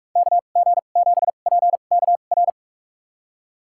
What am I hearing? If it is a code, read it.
Morse code, heard as a keyed tone, about 46 wpm, spaced as for 32 wpm, 700 Hz. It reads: KG7PKR
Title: USA Call Signs - 6 Characters - 2x Character Spacing - 45wpm